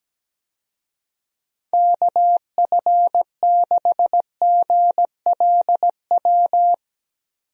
Morse code, heard as a keyed tone, about 17 wpm, 700 Hz. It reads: KF6GLW
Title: USA Call Signs - 6 Characters - 17wpm